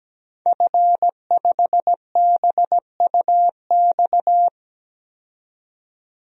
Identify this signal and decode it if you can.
Morse code, heard as a keyed tone, about 17 wpm, 700 Hz. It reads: F5BUX
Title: DXCC-100 Call Signs - 5 Characters - 17wpm